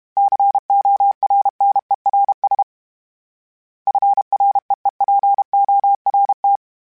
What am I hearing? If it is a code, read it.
Morse code, heard as a keyed tone, about 32 wpm, 800 Hz. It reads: CORNERS FREEPORT
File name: fromSMs / FromSMs mp3s / Cities@30